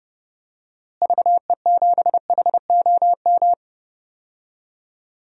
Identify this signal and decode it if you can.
Morse code, heard as a keyed tone, about 32 wpm, 700 Hz. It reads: VE7HOM
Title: DXCC-100 Call Signs - 6 Characters - Repeated 3x - 30wpm